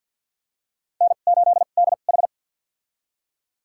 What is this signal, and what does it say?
Morse code, heard as a keyed tone, about 51 wpm, spaced as for 36 wpm, 700 Hz. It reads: N8DH
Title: USA Call Signs - 4 Characters - 2x Character Spacing - 50wpm